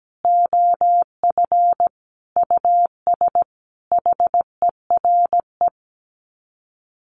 Morse code, heard as a keyed tone, about 17 wpm, 700 Hz. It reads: OF US HERE
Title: Sets of 5 Words 17wpm